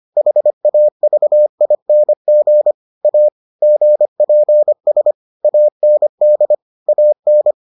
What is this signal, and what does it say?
Morse code, heard as a keyed tone, about 25 wpm, 600 Hz. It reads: HAVING A GPS AND AN